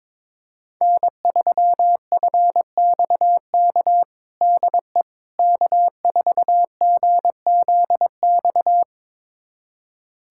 Morse code, heard as a keyed tone, about 21 wpm, 700 Hz. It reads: N3FXK DE K4GZX
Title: USA Callsign DE Callsign - 5 Characters - 22wpm